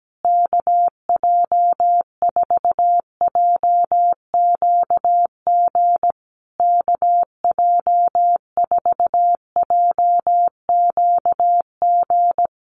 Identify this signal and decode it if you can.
Morse code, heard as a keyed tone, about 17 wpm, 700 Hz. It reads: KJ4JQG KJ4JQG
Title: USA Call Signs Repeated 3x 17wpm